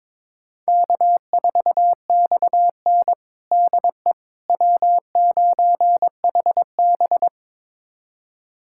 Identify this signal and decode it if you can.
Morse code, heard as a keyed tone, about 22 wpm, 700 Hz. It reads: K4XN DE W9HB
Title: USA Callsign DE Callsign - 4 Characters - 22wpm